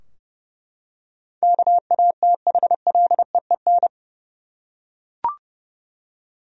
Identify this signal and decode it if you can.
Morse code, heard as a keyed tone, about 30 wpm, 700 Hz. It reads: KATHLEEN
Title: USA Common Names - 30wpm